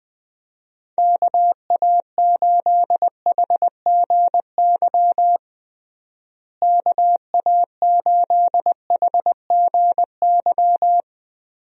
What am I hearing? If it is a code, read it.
Morse code, heard as a keyed tone, about 20 wpm, 700 Hz. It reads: KA8HGY KA8HGY
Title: USA Call Signs - 6 Characters - Repeated 2x - 20wpm